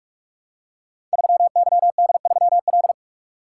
Morse code, heard as a keyed tone, about 44 wpm, 700 Hz. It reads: 3YD3L
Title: Code Groups - 5 Letter:Numbers - 45wpm